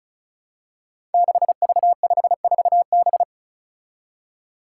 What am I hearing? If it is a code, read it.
Morse code, heard as a keyed tone, about 36 wpm, 700 Hz. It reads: 6V54B
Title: Five Challenging Letter-Number Combinations 35wpm